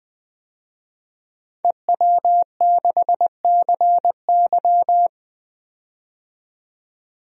Morse code, heard as a keyed tone, about 20 wpm, 700 Hz. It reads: EW6CY
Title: DXCC-100 Call Signs - 5 Characters - 20wpm